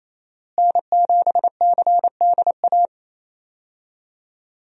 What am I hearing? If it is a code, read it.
Morse code, heard as a keyed tone, about 27 wpm, 700 Hz. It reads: N7CDA
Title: USA Call Signs - 5 Characters - Repeated 2x - 28wpm